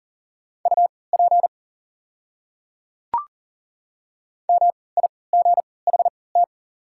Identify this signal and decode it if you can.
Morse code, heard as a keyed tone, about 38 wpm, spaced as for 22 wpm, 690 Hz. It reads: UP MIGHT
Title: 3x Character Spacing - Top 300 Words - 40wpm